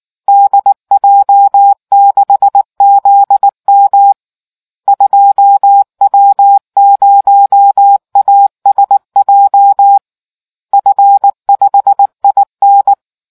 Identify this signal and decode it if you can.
Morse code, heard as a keyed tone, about 19 wpm, 800 Hz. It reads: DJ6ZM 2W0ASJ F5IN